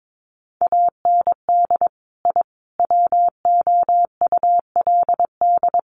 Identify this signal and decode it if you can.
Morse code, heard as a keyed tone, about 22 wpm, 700 Hz. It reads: AND I WOULD
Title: Sets of 5 Words Encore 22wpm